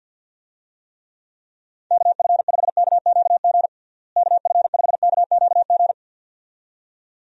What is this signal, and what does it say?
Morse code, heard as a keyed tone, about 50 wpm, 700 Hz. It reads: KF5KQG KF5KQG
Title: USA Call Signs - 6 Characters - Repeated 2x - 50wpm